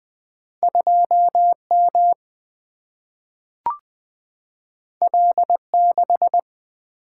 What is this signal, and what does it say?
Morse code, heard as a keyed tone, about 20 wpm, 700 Hz. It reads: 2 M L 6
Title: Single-Letter Single-Number 20wpm